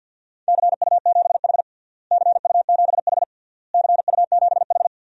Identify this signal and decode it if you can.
Morse code, heard as a keyed tone, about 52 wpm, 700 Hz. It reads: CU7H CU7H CU7H
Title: DXCC-100 Call Signs - 4 Characters - Repeated 3x - 50wpm